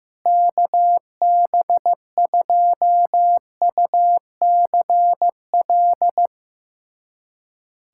Morse code, heard as a keyed tone, about 16 wpm, 700 Hz. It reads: KB2UCL
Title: USA Callsign DE Callsign 15wpm